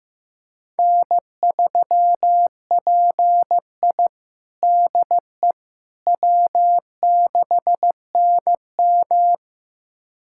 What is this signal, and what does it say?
Morse code, heard as a keyed tone, about 15 wpm, 700 Hz. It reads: N3PI DE W6NM
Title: USA Callsign DE Callsign - 4 Characters - 15wpm